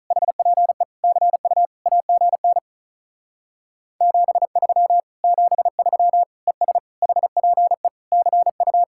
Morse code, heard as a keyed tone, about 43 wpm, 700 Hz. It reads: HPE CU AGN 73 73 ES HPE CU
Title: Speed-Racing Common QSO Phrases - 35wpm